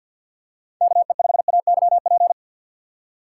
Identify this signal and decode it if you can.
Morse code, heard as a keyed tone, about 52 wpm, 700 Hz. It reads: KE5AYP
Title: USA Call Signs - 6 Characters - 3x Character Spacing - 50wpm